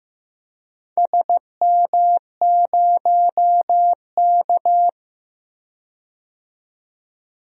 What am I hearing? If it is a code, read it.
Morse code, heard as a keyed tone, about 15 wpm, 700 Hz. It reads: SM0K
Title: DXCC-100 Call Signs - 4 Characters - 15wpm